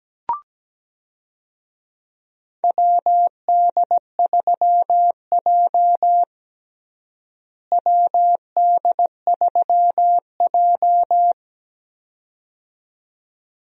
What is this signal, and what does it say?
Morse code, heard as a keyed tone, about 17 wpm, 700 Hz. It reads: WD3J WD3J
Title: USA Call Signs - 4 Characters - Repeated 2x - 17wpm